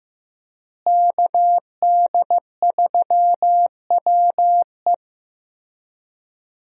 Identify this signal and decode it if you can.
Morse code, heard as a keyed tone, about 15 wpm, 700 Hz. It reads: KD3WE
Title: USA Call Signs - 5 Characters - Repeated 3x - 15wpm